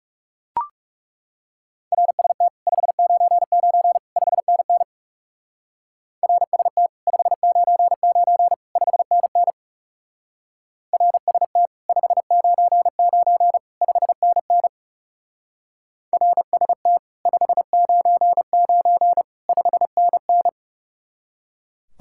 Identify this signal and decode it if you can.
Morse code, heard as a keyed tone, about 37 wpm, 700 Hz. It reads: RST 599 5NN RST 599 5NN RST 599 5NN RST 599 5NN
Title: Speed-Racing Common QSO Phrases - 30wpm